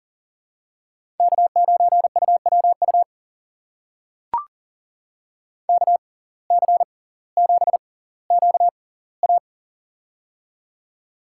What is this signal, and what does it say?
Morse code, heard as a keyed tone, about 40 wpm, spaced as for 20 wpm, 700 Hz. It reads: K9UWU KC7QA
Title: USA Call Signs - 5 Characters - 6x Character Spacing - 40wpm